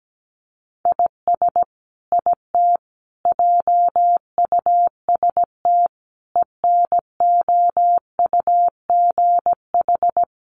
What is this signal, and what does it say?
Morse code, heard as a keyed tone, about 17 wpm, 700 Hz. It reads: IS IT JUST ENOUGH
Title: Sentences from Top 1500 Words 17wpm